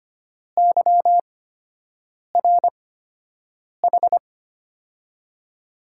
Morse code, heard as a keyed tone, about 25 wpm, spaced as for 7 wpm, 700 Hz. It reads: YRH
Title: USA Call Signs - 6 Characters - 8x Character Spacing - 25wpm